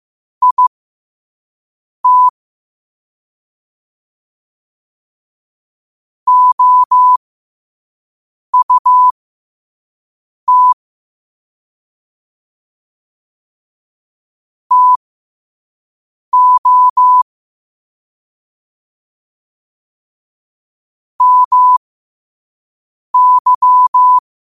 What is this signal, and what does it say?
Morse code, heard as a keyed tone, about 15 wpm, spaced as for 5 wpm, 1000 Hz. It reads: IT OUT TO MY
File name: qotdfiles / 5wpm